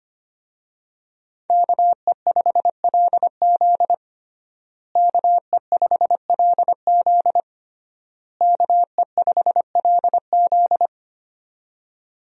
Morse code, heard as a keyed tone, about 25 wpm, 700 Hz. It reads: KE5LZ KE5LZ KE5LZ
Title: USA Call Signs - 5 Characters - Repeated 3x - 25wpm